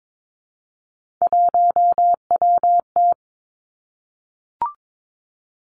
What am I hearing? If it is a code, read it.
Morse code, heard as a keyed tone, about 22 wpm, 700 Hz. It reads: 1WT
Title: Three Challenging Letter-Number Combinations 22wpm